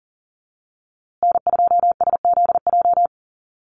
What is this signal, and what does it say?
Morse code, heard as a keyed tone, about 40 wpm, 700 Hz. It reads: N2SZJ
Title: USA Call Signs Repeated 2x 40wpm